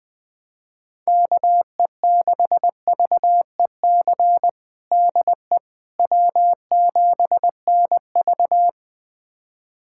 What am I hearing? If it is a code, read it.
Morse code, heard as a keyed tone, about 20 wpm, 700 Hz. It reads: KE6VEC DE W7NV
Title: USA Callsign DE Callsign 20wpm